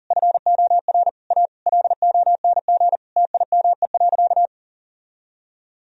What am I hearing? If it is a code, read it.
Morse code, heard as a keyed tone, about 40 wpm, 700 Hz. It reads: FOR A LONG TIME.
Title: Sentences from Top 1000 Words - Review Part 2 - 40wpm